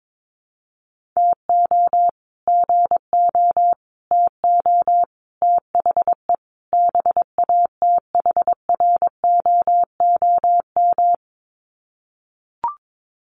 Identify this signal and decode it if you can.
Morse code, heard as a keyed tone, about 22 wpm, 700 Hz. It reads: TO GO TO THE BATHROOM
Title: Sets of 5 Words Encore 22wpm